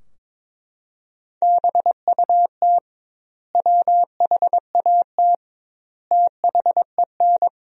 Morse code, heard as a keyed tone, about 21 wpm, 700 Hz. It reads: BUT WHAT THEN
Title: 2x Word-Spacing - Sentences from Top 500 Words - 22wpm